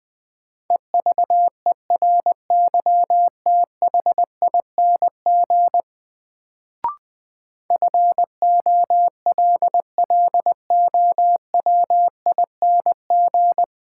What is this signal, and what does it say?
Morse code, heard as a keyed tone, about 20 wpm, 700 Hz. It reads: EVERYTHING FOLLOWING
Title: Sentences from Top 700 Words - New Word Drill - 20wpm